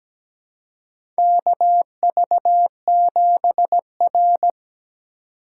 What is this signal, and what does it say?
Morse code, heard as a keyed tone, about 17 wpm, 700 Hz. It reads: KV7R
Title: USA Call Signs - 4 Characters - 4x Character Spacing - 17wpm